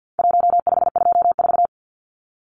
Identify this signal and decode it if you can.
Morse code, heard as a keyed tone, about 48 wpm, 700 Hz. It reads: 1524
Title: Numbers - 4-Digit 50wpm